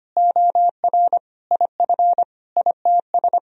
Morse code, heard as a keyed tone, about 25 wpm, 700 Hz. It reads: OR IF ITS
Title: Sentences from Top 1000 Words - Review Part 3 - 25wpm